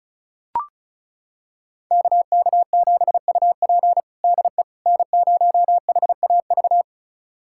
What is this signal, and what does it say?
Morse code, heard as a keyed tone, about 36 wpm, 700 Hz. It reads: KK7UP DE N0HAV
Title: USA Callsign DE Callsign - 5 Characters - 35wpm